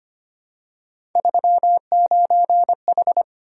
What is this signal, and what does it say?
Morse code, heard as a keyed tone, about 25 wpm, 700 Hz. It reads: 39H
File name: Three Letter-Number Combinations - 25wpm